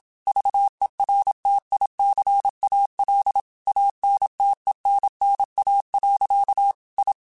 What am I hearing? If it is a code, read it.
Morse code, heard as a keyed tone, about 27 wpm, 800 Hz. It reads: VERTICAL ANTENNA. I